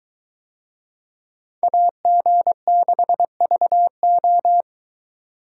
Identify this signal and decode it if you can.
Morse code, heard as a keyed tone, about 22 wpm, 700 Hz. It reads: AG6VO